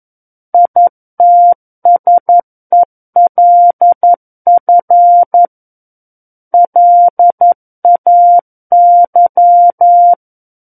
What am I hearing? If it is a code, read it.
Morse code, heard as a keyed tone, about 11 wpm, 700 Hz. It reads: ITSELF LAY
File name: 11 WPM